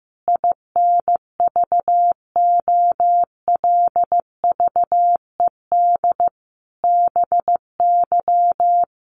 Morse code, heard as a keyed tone, about 15 wpm, 700 Hz. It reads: INVOLVED BY
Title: Sentences from Top 800 Words 15wpm